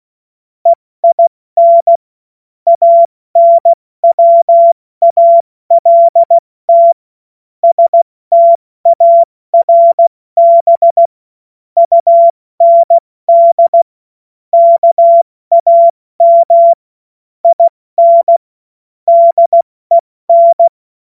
Morse code, heard as a keyed tone, about 16 wpm, 680 Hz. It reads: EIN ANWALT STARB UND KAM IN DEN
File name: Morsekurs / Witze / Witz 30. August 2022 70 F80